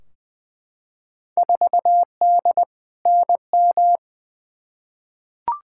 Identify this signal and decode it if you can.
Morse code, heard as a keyed tone, about 20 wpm, 700 Hz. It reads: 4D NM
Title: Field Day Exchanges Encore - 20wpm